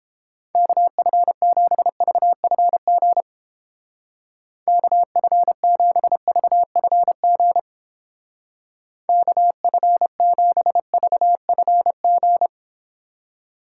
Morse code, heard as a keyed tone, about 29 wpm, 700 Hz. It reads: KF7VFG KF7VFG KF7VFG